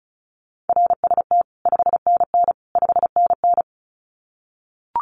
Morse code, heard as a keyed tone, about 34 wpm, 700 Hz. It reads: RST 5NN 5NN
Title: Common QSO Phrases Encore 35wpm